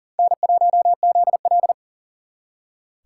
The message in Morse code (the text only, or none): N1ZL